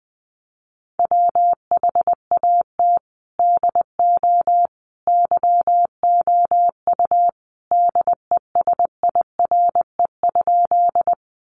Morse code, heard as a keyed tone, about 19 wpm, 700 Hz. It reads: WHAT DO YOU DESIRE?
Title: Sentences from Top 1100 Words 20wpm